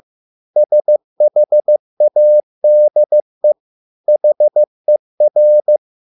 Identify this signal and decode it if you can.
Morse code, heard as a keyed tone, about 15 wpm, 600 Hz. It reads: SHADE HER